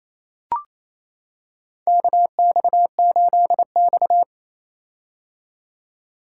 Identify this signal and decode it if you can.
Morse code, heard as a keyed tone, about 28 wpm, 700 Hz. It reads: KX8X
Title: K1USN SST QSO Practice - 28wpm